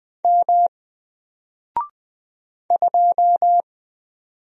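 Morse code, heard as a keyed tone, about 20 wpm, 700 Hz. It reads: M 2
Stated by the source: Single letter and number Morse drill at 20 wpm, from N4LCD.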